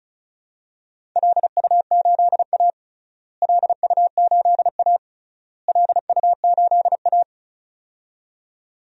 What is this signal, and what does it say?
Morse code, heard as a keyed tone, about 34 wpm, 700 Hz. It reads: LU8A LU8A LU8A
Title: DXCC-100 Call Signs - 4 Characters - Repeated 3x - 35wpm